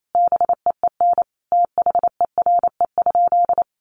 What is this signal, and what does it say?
Morse code, heard as a keyed tone, about 28 wpm, 700 Hz. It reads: BEEN THERE?
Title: Sentences from Top 100 Words 28wpm